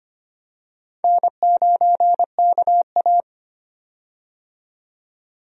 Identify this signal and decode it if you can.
Morse code, heard as a keyed tone, about 25 wpm, 700 Hz. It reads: N9KA
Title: USA Call Signs - 4 Characters - 25wpm